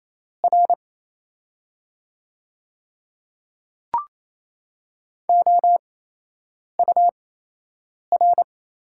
Morse code, heard as a keyed tone, about 28 wpm, spaced as for 8 wpm, 690 Hz. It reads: R OUR
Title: 8x Character Spacing - Top 500 Words - 28wpm